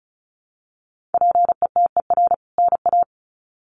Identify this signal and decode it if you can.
Morse code, heard as a keyed tone, about 34 wpm, 700 Hz. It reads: PETER NA
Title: CWT Exchange 35wpm